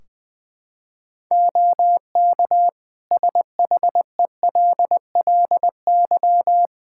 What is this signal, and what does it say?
Morse code, heard as a keyed tone, about 20 wpm, 700 Hz. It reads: OK SHELLY
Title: Practice QSO - Exchange 3 - More Info - 20wpm